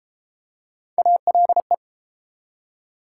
Morse code, heard as a keyed tone, about 29 wpm, 700 Hz. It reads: ALE